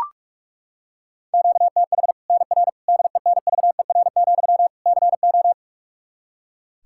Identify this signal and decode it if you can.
Morse code, heard as a keyed tone, about 42 wpm, 700 Hz. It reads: QTH NR DENVER, CO
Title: K1USN SST QSO Practice 45wpm